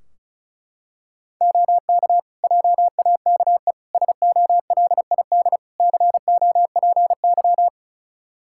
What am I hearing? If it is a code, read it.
Morse code, heard as a keyed tone, about 36 wpm, 700 Hz. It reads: OK JAKE SOLID COPY
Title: Practice QSO - Exchange 3 - More Info - 35wpm